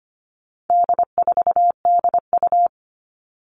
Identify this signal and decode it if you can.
Morse code, heard as a keyed tone, about 25 wpm, 700 Hz. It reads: D4DU
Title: All Cut Numbers - 4-Digit 25wpm